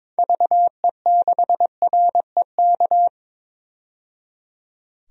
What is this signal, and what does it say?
Morse code, heard as a keyed tone, about 21 wpm, 700 Hz. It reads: VE6REK
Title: DXCC-100 Call Signs - 6 Characters - 22wpm